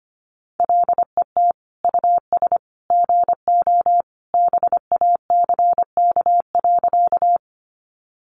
Morse code, heard as a keyed tone, about 25 wpm, 700 Hz. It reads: LET US GO BACK.
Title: Sentences from Top 300 Words 25wpm